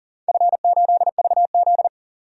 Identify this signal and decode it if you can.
Morse code, heard as a keyed tone, about 40 wpm, 700 Hz. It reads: F8VZ
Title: DXCC-100 Call Signs - 4 Characters - 40wpm